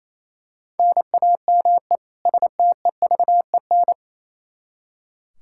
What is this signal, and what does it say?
Morse code, heard as a keyed tone, about 28 wpm, 700 Hz. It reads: NAME STEVEN
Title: Practice QSO - Exchange 1 - Beginning - 28wpm